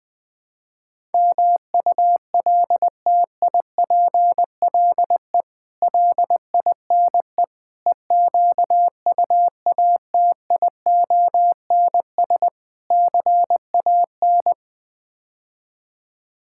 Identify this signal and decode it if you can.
Morse code, heard as a keyed tone, about 21 wpm, 700 Hz. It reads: MULTIPLE LINE EQUATIONS CAN
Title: Build A Phrase - 5-Word Phrases - 20wpm